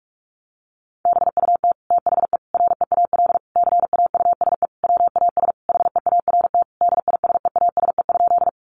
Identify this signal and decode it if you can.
Morse code, heard as a keyed tone, about 44 wpm, 700 Hz. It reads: BUT THE REAL CAUSE WAS HEART DISEASE?
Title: Sentences from Top 1200 Words 45wpm